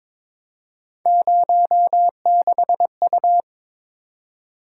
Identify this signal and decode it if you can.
Morse code, heard as a keyed tone, about 22 wpm, 700 Hz. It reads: 06U
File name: practice / 22wpm / Three Letter-Number Combinations - 22wpm